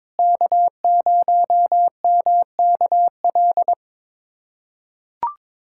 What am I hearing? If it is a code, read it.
Morse code, heard as a keyed tone, about 22 wpm, 700 Hz. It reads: K0MKL
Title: K1USN SST QSO Practice - 22wpm